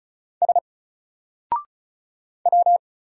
Morse code, heard as a keyed tone, about 34 wpm, 700 Hz. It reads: S W
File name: Single-Letter-Number-35wpm